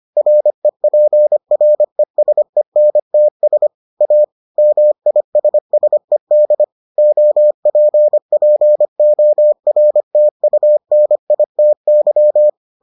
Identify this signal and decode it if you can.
Morse code, heard as a keyed tone, about 25 wpm, 600 Hz. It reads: REPRESENTS A MISSED OPPORTUNITY